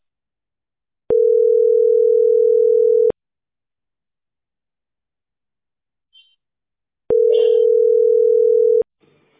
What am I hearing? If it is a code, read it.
Ringback tone (North America), 440 + 480 Hz.